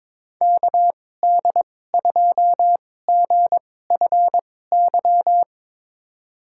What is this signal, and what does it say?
Morse code, heard as a keyed tone, about 21 wpm, spaced as for 16 wpm, 700 Hz. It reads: KD2GFY
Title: USA Call Signs - 6 Characters - 2x Character Spacing - 22wpm